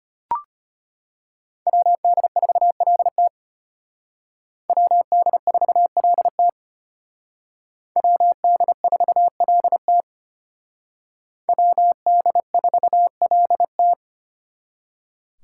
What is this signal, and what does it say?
Morse code, heard as a keyed tone, about 32 wpm, 700 Hz. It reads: WD4LT WD4LT WD4LT WD4LT